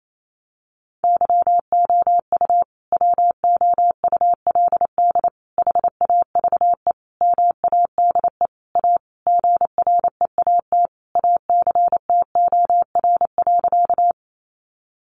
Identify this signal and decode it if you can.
Morse code, heard as a keyed tone, about 28 wpm, 700 Hz. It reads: YOU WOULD HAVE MADE A GREAT ACTOR.
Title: Sentences from Top 1500 Words 28wpm